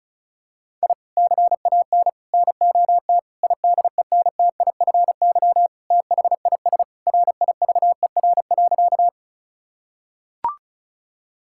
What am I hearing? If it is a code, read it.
Morse code, heard as a keyed tone, about 36 wpm, 700 Hz. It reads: I CAN NOT IDENTIFY THIS RIVER.
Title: Sentences from Top 1000 Words - Review Part 6 - 35wpm